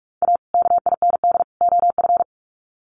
Morse code, heard as a keyed tone, about 44 wpm, 700 Hz. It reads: A KIND OF
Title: Sets of 3 Words 45wpm